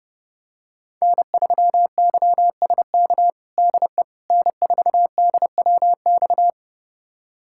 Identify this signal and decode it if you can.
Morse code, heard as a keyed tone, about 30 wpm, 700 Hz. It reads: N3YSK DE N4DWX
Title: USA Callsign DE Callsign - 5 Characters - 30wpm